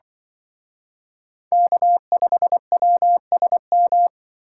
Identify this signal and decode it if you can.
Morse code, heard as a keyed tone, about 25 wpm, 700 Hz. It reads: K5WSM
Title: Speed Racing - USA 1x3 Call Signs 20wpm